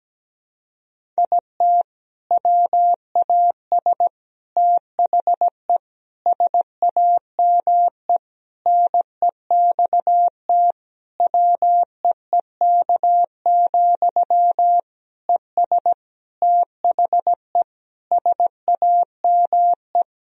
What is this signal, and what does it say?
Morse code, heard as a keyed tone, about 17 wpm, 700 Hz. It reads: IT WAS THE SAME NEXT WEEK, ES THE SAME
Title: Sentences from Top 500 Words and Abbreviations 17wpm